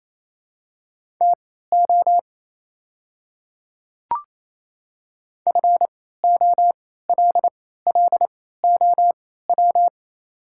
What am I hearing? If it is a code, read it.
Morse code, heard as a keyed tone, about 27 wpm, spaced as for 16 wpm, 700 Hz. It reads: TO FOLLOW
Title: 3x Character Spacing - Top 300 Words - 28wpm